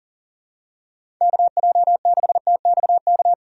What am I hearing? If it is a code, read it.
Morse code, heard as a keyed tone, about 40 wpm, 700 Hz. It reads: KJ6TXK
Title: USA Call Signs - 6 Characters - 4x Character Spacing - 40wpm